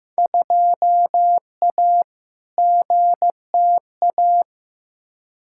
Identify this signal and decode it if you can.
Morse code, heard as a keyed tone, about 15 wpm, 700 Hz. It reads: 2A GTA
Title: Field Day Exchanges Encore - 15wpm